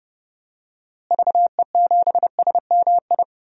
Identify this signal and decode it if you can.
Morse code, heard as a keyed tone, about 30 wpm, 700 Hz. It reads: VE7SMI
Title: DXCC-100 Call Signs - 6 Characters - Repeated 2x - 30wpm